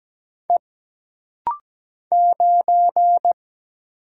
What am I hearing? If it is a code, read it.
Morse code, heard as a keyed tone, about 17 wpm, 700 Hz. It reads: E 9
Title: Single Letter-Number Alt - Rapid-Fire 17wpm